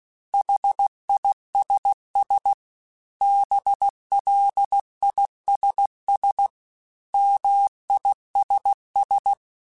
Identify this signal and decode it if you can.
Morse code, heard as a keyed tone, about 16 wpm, 800 Hz. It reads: HISS BLISS MISS
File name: General / General-mp3s / ISS @15